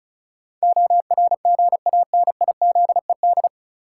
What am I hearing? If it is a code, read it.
Morse code, heard as a keyed tone, about 36 wpm, 700 Hz. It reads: ORGANIZED
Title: Sentences from Top 1300 Words - New Word Drill Alt - 35wpm